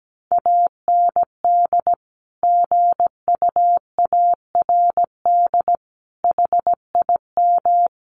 Morse code, heard as a keyed tone, about 17 wpm, 700 Hz. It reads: AND GUARD HIM
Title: Sentences from Top 1400 Words 17wpm